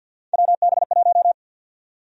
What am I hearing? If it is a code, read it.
Morse code, heard as a keyed tone, about 52 wpm, 700 Hz. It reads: WB1
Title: USA Call Sign Prefixes - 50wpm